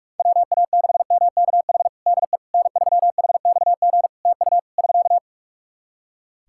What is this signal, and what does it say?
Morse code, heard as a keyed tone, about 46 wpm, 700 Hz. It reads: WA6MKH DE N3HXG TU <SK>
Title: Practice QSO - Exchange 4 - Ending - 45wpm